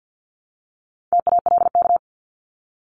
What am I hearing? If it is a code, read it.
Morse code, heard as a keyed tone, about 50 wpm, 700 Hz. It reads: TALK
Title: Top 500 Words Repeated 50wpm